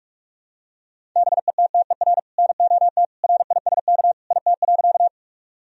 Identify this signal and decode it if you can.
Morse code, heard as a keyed tone, about 46 wpm, 700 Hz. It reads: BETTER NOT RISK IT.
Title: Sentences from Top 1000 Words - Review Part 4 - 45wpm